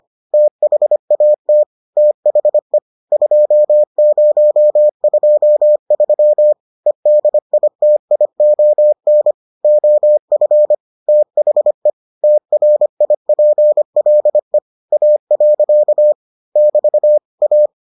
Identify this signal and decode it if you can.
Morse code, heard as a keyed tone, about 25 wpm, 600 Hz. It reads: THAT THE 2023 EDITION OF THE TRIPLE A. <BT> A